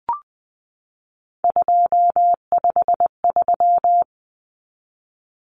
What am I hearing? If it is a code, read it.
Morse code, heard as a keyed tone, about 20 wpm, 700 Hz. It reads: 253
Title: Numbers - 3-Digit 20wpm